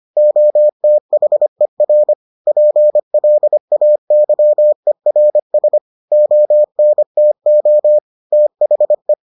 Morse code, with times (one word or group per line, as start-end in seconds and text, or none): 0.2-2.1 OTHER
2.5-5.8 PLAYERS
6.1-8.0 ONTO
8.3-9.1 THE